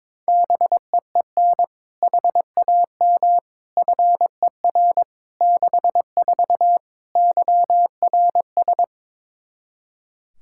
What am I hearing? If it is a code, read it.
Morse code, heard as a keyed tone, about 22 wpm, 700 Hz. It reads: BEEN HAM FER 64 YRS
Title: K1USN SST QSO Practice 22wpm